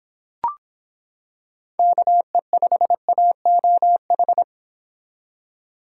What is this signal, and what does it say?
Morse code, heard as a keyed tone, about 26 wpm, 700 Hz. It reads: KE5AOH